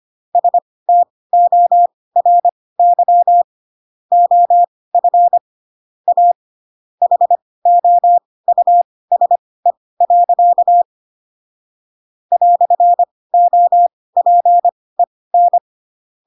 Morse code, heard as a keyed tone, about 25 wpm, spaced as for 18 wpm, 700 Hz. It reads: STORY OF A HOUSE. "OPEN
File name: Pinocchio / Pinocchio ch7 12 25 ik1hge